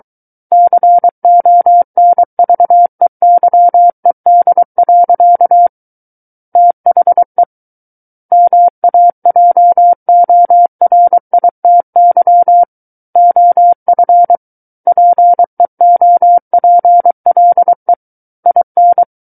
Morse code, heard as a keyed tone, about 23 wpm, 700 Hz. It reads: CONVEYED. THE MAJORITY OF PEOPLE IN
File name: Book One / 14-Chapter 14 - 23 WPM